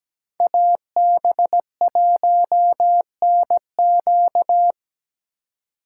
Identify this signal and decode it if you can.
Morse code, heard as a keyed tone, about 20 wpm, 700 Hz. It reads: AB1NQ
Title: Speed Racing - USA 2x2 Call Signs 17wpm